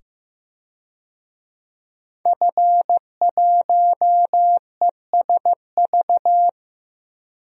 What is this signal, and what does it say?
Morse code, heard as a keyed tone, about 15 wpm, 700 Hz. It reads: F1ESV
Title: DXCC-100 Call Signs - 5 Characters - 15wpm